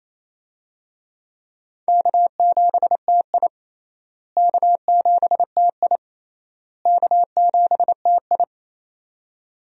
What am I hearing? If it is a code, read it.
Morse code, heard as a keyed tone, about 28 wpm, 700 Hz. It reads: K7TI K7TI K7TI
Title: USA Call Signs - 4 Characters - Repeated 3x - 28wpm